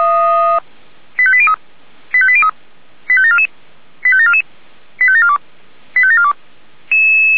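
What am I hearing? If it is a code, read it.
Five-tone selective call (ZVEI), heard as quick runs of five tones, tones of about 70 ms. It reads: 85803 85803 85744 85744 85742 85742
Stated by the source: ZVEI five-tone selective call received on an SDR.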